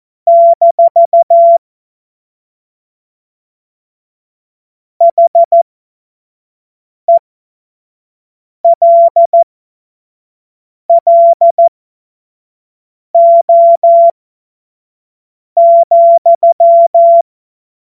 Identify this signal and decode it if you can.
Morse code, heard as a keyed tone, about 14 wpm, spaced as for 5 wpm, 680 Hz. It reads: - HELLO,
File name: Morsekurs / Witze / Witz 01. März 2023 25 F70